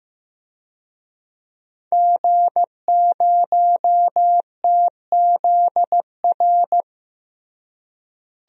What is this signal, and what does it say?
Morse code, heard as a keyed tone, about 15 wpm, 700 Hz. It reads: G0TZR
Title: DXCC-100 Call Signs - 5 Characters - Repeated 2x - 15wpm